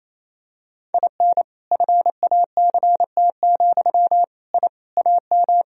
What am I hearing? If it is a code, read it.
Morse code, heard as a keyed tone, about 28 wpm, 700 Hz. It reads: IN FACT, I AM
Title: Sentences from Top 1000 Words - Review Part 3 - 28wpm